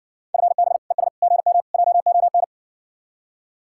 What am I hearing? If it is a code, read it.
Morse code, heard as a keyed tone, about 60 wpm, 700 Hz. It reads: FB ES GM JON